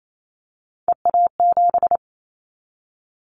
Morse code, heard as a keyed tone, about 28 wpm, 700 Hz. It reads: EA7
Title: Callsign Prefixes for Top 25 DXCC - Repeated 3x - 28wpm